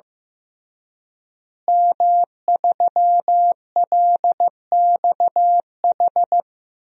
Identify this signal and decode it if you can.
Morse code, heard as a keyed tone, about 15 wpm, 700 Hz. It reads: M3LXH
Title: DXCC-100 Call Signs - 5 Characters - Repeated 2x - 15wpm